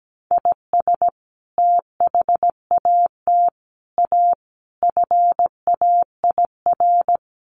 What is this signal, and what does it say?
Morse code, heard as a keyed tone, about 17 wpm, 700 Hz. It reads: IS THAT A FAIR
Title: Sentences from Top 1300 Words 17wpm